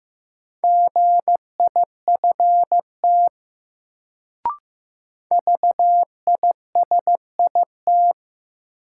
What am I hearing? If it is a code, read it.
Morse code, heard as a keyed tone, about 15 wpm, 700 Hz. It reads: GIFT VISIT
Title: Sentences from Top 900 Words - New Word Drill - 15wpm